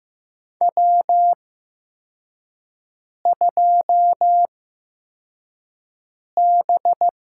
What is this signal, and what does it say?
Morse code, heard as a keyed tone, about 15 wpm, spaced as for 4 wpm, 700 Hz. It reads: W2B
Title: USA Call Signs - 4 Characters - 8x Character Spacing - 15wpm